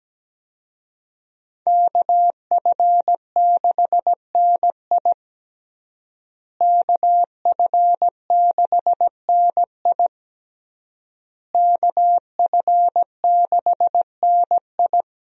Morse code, heard as a keyed tone, about 17 wpm, 700 Hz. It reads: KF6NI KF6NI KF6NI
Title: USA Call Signs - 5 Characters - Repeated 3x - 17wpm